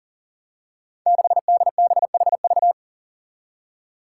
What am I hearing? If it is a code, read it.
Morse code, heard as a keyed tone, about 43 wpm, 700 Hz. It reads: 6DBHV
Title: Five Challenging Letter-Number Combinations 40wpm